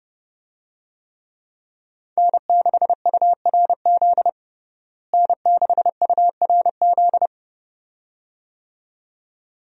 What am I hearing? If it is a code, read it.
Morse code, heard as a keyed tone, about 30 wpm, 700 Hz. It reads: N6URZ N6URZ
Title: USA Call Signs - 5 Characters - Repeated 2x - 30wpm